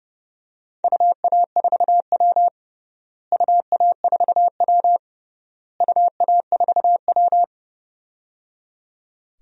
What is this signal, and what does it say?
Morse code, heard as a keyed tone, about 30 wpm, 700 Hz. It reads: UA4W UA4W UA4W
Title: DXCC-100 Call Signs - 4 Characters - Repeated 3x - 30wpm